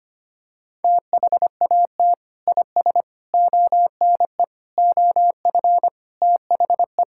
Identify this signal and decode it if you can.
Morse code, heard as a keyed tone, about 25 wpm, 700 Hz. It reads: THAT IS ONE OF THE
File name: practice / 25wpm / Sentences with 4-Word Phrases - 25wpm